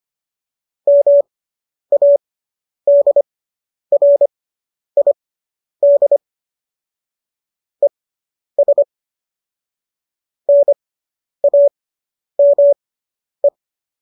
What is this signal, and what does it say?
Morse code, heard as a keyed tone, about 25 wpm, spaced as for 10 wpm, 570 Hz. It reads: MADRID ES NAME